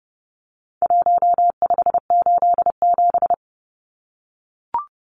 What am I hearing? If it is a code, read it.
Morse code, heard as a keyed tone, about 30 wpm, 700 Hz. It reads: 1587
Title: Numbers - Contest Style - 4-Digit 30wpm